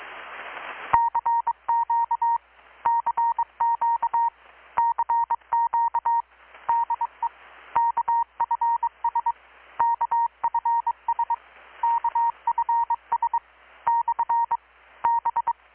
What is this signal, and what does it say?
Morse code, heard as a keyed tone, about 21 wpm, 930 Hz. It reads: CQ CQ CQ DE KFS KFS KFS / B
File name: blog / swl / 6-abril-2019-12695-USB-KPS